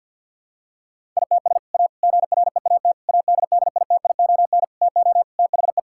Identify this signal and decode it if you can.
Morse code, heard as a keyed tone, about 50 wpm, 700 Hz. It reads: ITS A GREAT ADDITION TO THE
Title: Sentences from Top 1000 Words - Review Part 6 - 50wpm